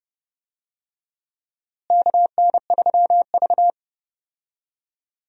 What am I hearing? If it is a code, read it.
Morse code, heard as a keyed tone, about 31 wpm, 700 Hz. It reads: KN3V
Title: USA Call Signs - 4 Characters - Repeated 3x - 30wpm